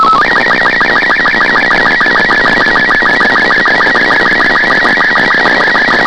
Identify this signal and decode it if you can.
SSTV (slow-scan television), Robot 36.
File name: kd4dla / mir1